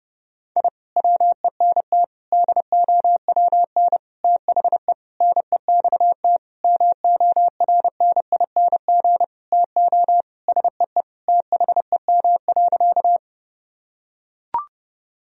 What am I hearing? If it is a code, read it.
Morse code, heard as a keyed tone, about 30 wpm, 700 Hz. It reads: I WENT DOWN THE NEXT MORNING TO SEE THEM.
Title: Sentences from Top 1000 Words - Review Part 1 - 30wpm